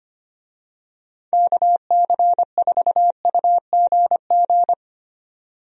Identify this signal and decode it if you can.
Morse code, heard as a keyed tone, about 25 wpm, 700 Hz. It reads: KC4UGG
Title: USA Call Signs - 6 Characters - 25wpm